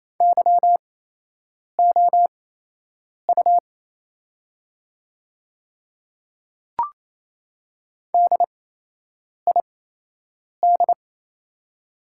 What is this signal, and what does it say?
Morse code, heard as a keyed tone, about 28 wpm, spaced as for 8 wpm, 700 Hz. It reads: YOU DID
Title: 8x Character Spacing - Top 100 Words - 28wpm